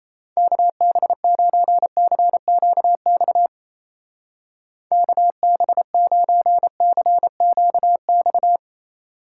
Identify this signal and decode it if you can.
Morse code, heard as a keyed tone, about 36 wpm, 700 Hz. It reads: KB9CQX KB9CQX